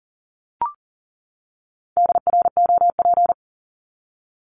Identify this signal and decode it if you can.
Morse code, heard as a keyed tone, about 40 wpm, 700 Hz. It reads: DROP
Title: Top 1251-1500 Words 40wpm